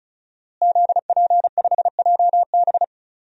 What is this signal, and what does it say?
Morse code, heard as a keyed tone, about 36 wpm, 700 Hz. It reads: ZP5JB
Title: DXCC-100 Call Signs - 5 Characters - Repeated 2x - 35wpm